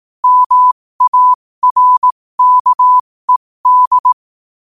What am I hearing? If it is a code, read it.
Morse code, heard as a keyed tone, about 18 wpm, spaced as for 15 wpm, 1000 Hz. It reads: MARKED
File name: qotdfiles / 15wpm